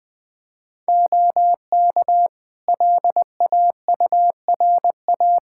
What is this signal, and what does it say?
Morse code, heard as a keyed tone, about 21 wpm, 700 Hz. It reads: OK LAURA
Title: K1USN SST QSO Practice 20wpm